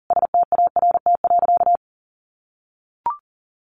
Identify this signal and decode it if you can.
Morse code, heard as a keyed tone, about 40 wpm, 700 Hz. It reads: START.
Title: Sentences from Top 300 Words 40wpm